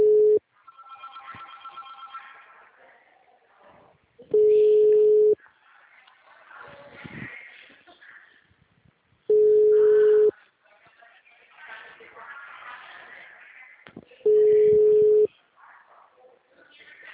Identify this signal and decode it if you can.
Ringback tone (Europe, 425 Hz).
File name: out- -2048- - - .50870